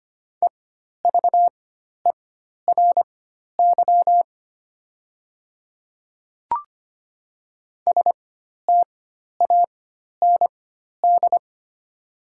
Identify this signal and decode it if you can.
Morse code, heard as a keyed tone, about 25 wpm, spaced as for 12 wpm, 700 Hz. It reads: EVERY STAND
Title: 4x Character Spacing - Top 300 Words - 25wpm